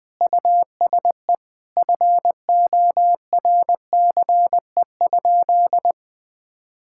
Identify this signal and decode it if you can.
Morse code, heard as a keyed tone, about 20 wpm, 700 Hz. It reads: USE FORCE?
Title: Sentences from Top 1600 Words 20wpm